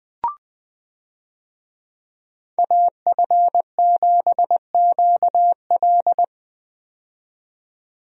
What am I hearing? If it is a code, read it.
Morse code, heard as a keyed tone, about 20 wpm, 700 Hz. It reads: AF7QL
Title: USA Call Signs - 5 Characters - 20wpm